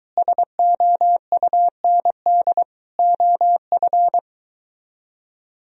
Morse code, heard as a keyed tone, about 22 wpm, 700 Hz. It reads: SOUND OF